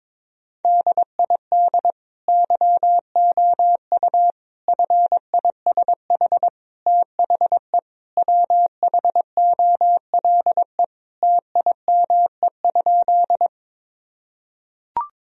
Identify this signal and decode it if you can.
Morse code, heard as a keyed tone, about 22 wpm, 700 Hz. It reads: DID YOU FISH THE WHOLE TIME?
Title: Sentences from Top 1000 Words - Review Part 1 - 22wpm